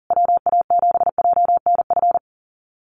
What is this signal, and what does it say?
Morse code, heard as a keyed tone, about 40 wpm, 700 Hz. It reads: WA7JNF
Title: USA Call Signs Repeated 3x 40wpm